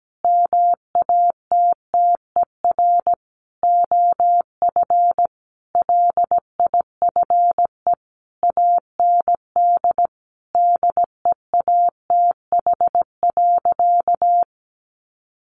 Morse code, heard as a keyed tone, about 17 wpm, 700 Hz. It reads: MATTER OF LIFE AND DEATH.
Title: Sentences from Top 1100 Words 17wpm